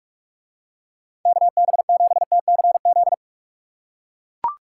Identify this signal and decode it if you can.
Morse code, heard as a keyed tone, about 38 wpm, 700 Hz. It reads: KB7TCZ